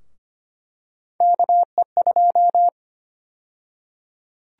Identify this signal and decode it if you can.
Morse code, heard as a keyed tone, about 25 wpm, 700 Hz. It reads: KE2
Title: USA Call Sign Prefixes - 25wpm